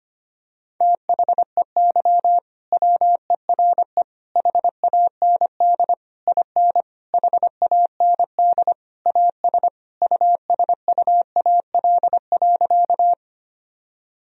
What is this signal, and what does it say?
Morse code, heard as a keyed tone, about 25 wpm, 700 Hz. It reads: THEY WERE HAND IN HAND AS USUAL.
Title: Sentences from Top 1000 Words - Review Part 2 - 25wpm